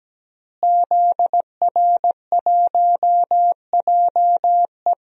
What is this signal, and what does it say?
Morse code, heard as a keyed tone, about 17 wpm, 700 Hz. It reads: ZR1JE
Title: DXCC-100 Call Signs - 5 Characters - Repeated 3x - 17wpm